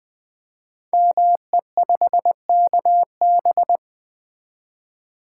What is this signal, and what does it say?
Morse code, heard as a keyed tone, about 20 wpm, 700 Hz. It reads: ME5KB
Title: Code Groups - 5 Letter:Numbers - 20wpm